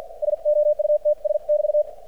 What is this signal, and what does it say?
Morse code, heard as a keyed tone, about 50 wpm, spaced as for 43 wpm, 600 Hz. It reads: SOUTS<BT>
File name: archived / CRSnet